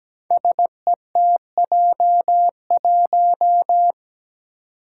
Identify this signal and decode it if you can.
Morse code, heard as a keyed tone, about 17 wpm, 700 Hz. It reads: SETJ1
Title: Five Challenging Letter-Number Combinations 17wpm